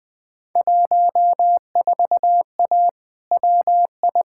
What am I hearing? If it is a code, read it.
Morse code, heard as a keyed tone, about 20 wpm, 700 Hz. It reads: 14A WI
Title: Field Day Exchanges Encore - 20wpm